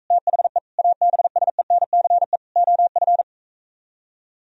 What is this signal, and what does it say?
Morse code, heard as a keyed tone, about 45 wpm, 700 Hz. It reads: THE ABSENCE OF